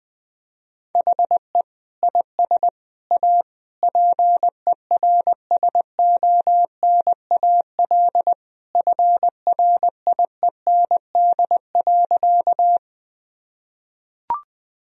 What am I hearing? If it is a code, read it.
Morse code, heard as a keyed tone, about 20 wpm, 700 Hz. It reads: HE IS A PERSONAL FRIEND.
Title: Sentences from Top 1000 Words - Review Part 4 - 20wpm